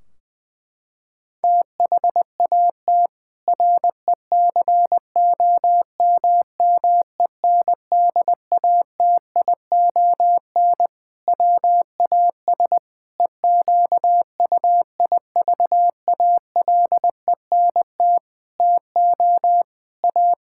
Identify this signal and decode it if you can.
Morse code, heard as a keyed tone, about 20 wpm, 700 Hz. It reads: THAT RECOMMENDATION WAS EQUIVALENT TO A
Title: Sentences from Top 1600 Words 20wpm